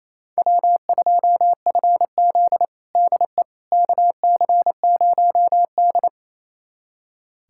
Morse code, heard as a keyed tone, about 28 wpm, 700 Hz. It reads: W2FZ DE KC0D
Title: USA Callsign DE Callsign 28wpm